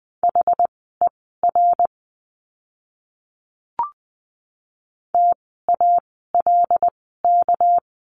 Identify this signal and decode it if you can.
Morse code, heard as a keyed tone, about 20 wpm, spaced as for 14 wpm, 700 Hz. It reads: HER TALK
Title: 244 Lesson 39f - Letter z - Words 20wpm